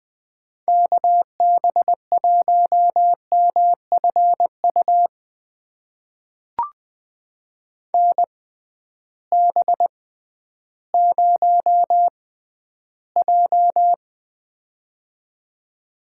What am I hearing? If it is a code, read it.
Morse code, heard as a keyed tone, about 20 wpm, spaced as for 12 wpm, 700 Hz. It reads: KB1MFU NB0J
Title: USA Call Signs - 6x Character Spacing - 20wpm